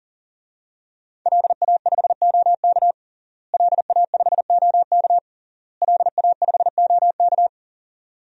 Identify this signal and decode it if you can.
Morse code, heard as a keyed tone, about 40 wpm, 700 Hz. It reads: LA5OK LA5OK LA5OK
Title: DXCC-100 Call Signs - 5 Characters - Repeated 3x - 40wpm